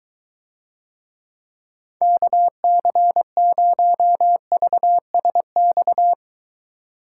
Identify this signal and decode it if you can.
Morse code, heard as a keyed tone, about 23 wpm, 700 Hz. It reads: KC0VSX